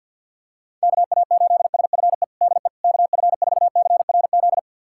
Morse code, heard as a keyed tone, about 52 wpm, 690 Hz. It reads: KA8SFE DE KF4CRZ
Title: USA Callsign DE Callsign - 6 Characters - 50wpm